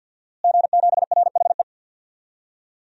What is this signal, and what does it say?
Morse code, heard as a keyed tone, about 52 wpm, 700 Hz. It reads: G7RHE
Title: DXCC-100 Call Signs - 5 Characters - Repeated 3x - 50wpm